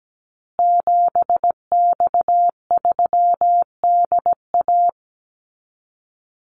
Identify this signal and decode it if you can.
Morse code, heard as a keyed tone, about 17 wpm, 700 Hz. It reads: 7X3DA
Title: Call Signs Repeated 3x - 17wpm